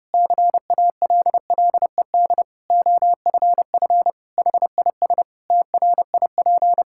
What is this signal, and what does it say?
Morse code, heard as a keyed tone, about 32 wpm, 700 Hz. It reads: CALLED OFF HIS TRIP
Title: Pangram Challenge v2 - 30wpm